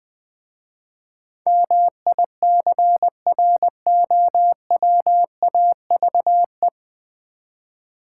Morse code, heard as a keyed tone, about 21 wpm, 700 Hz. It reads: MICROWAVE
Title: Easy Words with Hints - 20wpm